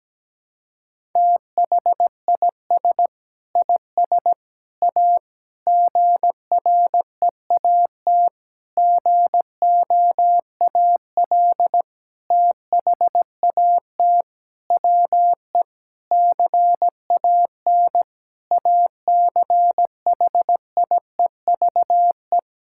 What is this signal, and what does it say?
Morse code, heard as a keyed tone, about 17 wpm, 700 Hz. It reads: THIS IS A GREAT GOAL THAT WE CAN ACHIEVE
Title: Sentences from Top 1000 Words - Review Part 6 - 17wpm